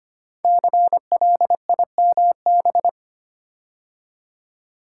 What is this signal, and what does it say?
Morse code, heard as a keyed tone, about 25 wpm, 700 Hz. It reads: CLIMB
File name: 5 Letter Words - First 3 Letters Given - 25wpm